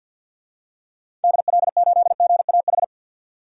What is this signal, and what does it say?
Morse code, heard as a keyed tone, about 50 wpm, 700 Hz. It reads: DL8GAH
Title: DXCC-100 Call Signs - 6 Characters - 50wpm